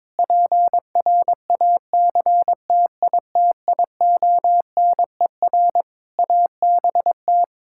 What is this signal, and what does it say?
Morse code, heard as a keyed tone, about 21 wpm, 700 Hz. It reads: PRACTITIONER ABT
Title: Practice QSO - Exchange 3 - More Info - 22wpm